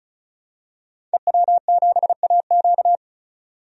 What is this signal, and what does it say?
Morse code, heard as a keyed tone, about 36 wpm, 700 Hz. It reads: EW7AQ
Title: DXCC-100 Call Signs - 5 Characters - 35wpm